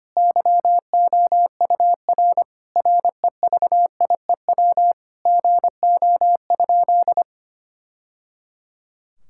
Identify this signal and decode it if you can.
Morse code, heard as a keyed tone, about 25 wpm, 700 Hz. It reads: YOUR REVIEW GO?
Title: Sentences from Top 1000 Words - Review Part 6 - 25wpm